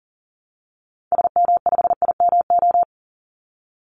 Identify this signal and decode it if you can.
Morse code, heard as a keyed tone, about 40 wpm, 700 Hz. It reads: SM5IMO
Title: Call Signs Repeated 3x - 40wpm